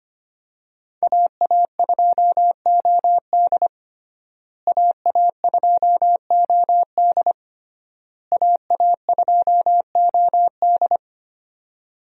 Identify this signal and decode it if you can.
Morse code, heard as a keyed tone, about 25 wpm, 700 Hz. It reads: AA2OD AA2OD AA2OD
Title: USA Call Signs - 5 Characters - Repeated 3x - 25wpm